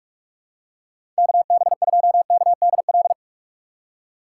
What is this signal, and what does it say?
Morse code, heard as a keyed tone, about 46 wpm, 700 Hz. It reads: KB2KDL
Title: USA Call Signs - 6 Characters - 3x Character Spacing - 45wpm